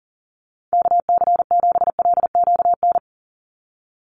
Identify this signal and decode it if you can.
Morse code, heard as a keyed tone, about 41 wpm, 700 Hz. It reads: KC7LQN
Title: USA Call Signs 40wpm